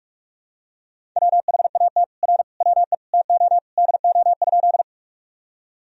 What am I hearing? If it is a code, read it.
Morse code, heard as a keyed tone, about 46 wpm, 700 Hz. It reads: WHAT R WE TO DO?
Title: Sentences from Top 100 Words and Abbreviations 45wpm